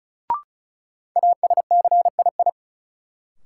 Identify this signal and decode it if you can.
Morse code, heard as a keyed tone, about 36 wpm, 700 Hz. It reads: ASCII
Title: Sets of 5 Words - New Word Drill - 35wpm